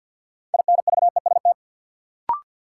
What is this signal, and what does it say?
Morse code, heard as a keyed tone, about 53 wpm, 700 Hz. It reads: INVEST
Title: Sentences from Top 1100 Words - New Word Drill Alt - 50wpm